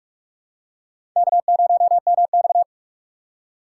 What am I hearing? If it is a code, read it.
Morse code, heard as a keyed tone, about 46 wpm, 700 Hz. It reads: K0MX
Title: USA Call Signs - 2x Character Spacing - 45wpm